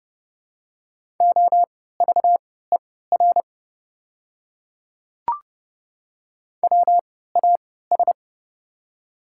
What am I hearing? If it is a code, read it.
Morse code, heard as a keyed tone, about 30 wpm, spaced as for 17 wpm, 700 Hz. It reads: OVER WAS
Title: 3x Character Spacing - Top 100 Words - 30wpm